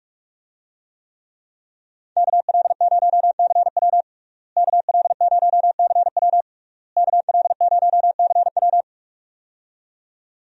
Morse code, heard as a keyed tone, about 46 wpm, 700 Hz. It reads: KL0CW KL0CW KL0CW
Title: USA Call Signs - 5 Characters - Repeated 3x - 45wpm